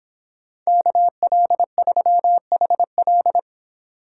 Morse code, heard as a keyed tone, about 30 wpm, 700 Hz. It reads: KL3HL